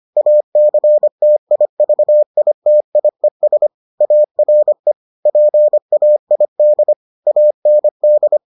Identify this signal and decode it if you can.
Morse code, heard as a keyed tone, about 25 wpm, 600 Hz. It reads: ACTIVITIES ARE PAID AND